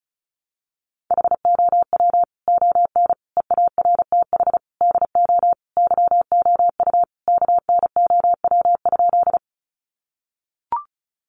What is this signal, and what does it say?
Morse code, heard as a keyed tone, about 34 wpm, 700 Hz. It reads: HOW ON EARTH DO YOU KNOW?
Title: Sentences from Top 200 Words 35wpm